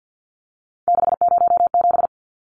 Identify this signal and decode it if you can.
Morse code, heard as a keyed tone, about 49 wpm, 700 Hz. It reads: 607
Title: Numbers - 3-Digit 50wpm